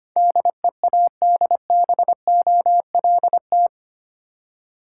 Morse code, heard as a keyed tone, about 25 wpm, 690 Hz. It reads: DEADBOLT
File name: Easy Words - Speed-Racing with Hints - 25wpm